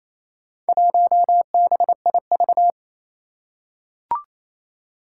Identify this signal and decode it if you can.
Morse code, heard as a keyed tone, about 28 wpm, 700 Hz. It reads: 1BIV
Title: Four Challenging Letter-Number Combinations 28wpm